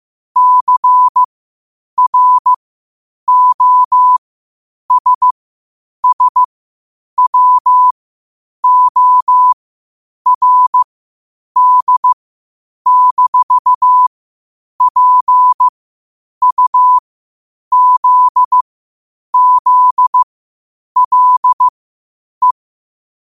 Morse code, heard as a keyed tone, about 15 wpm, spaced as for 8 wpm, 1000 Hz. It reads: CROSSWORD-PUZZLE
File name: qotdfiles / 8wpm